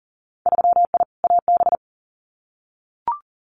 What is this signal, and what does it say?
Morse code, heard as a keyed tone, about 38 wpm, 700 Hz. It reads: 3I AB
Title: Winter Field Day Exchanges Encore 40wpm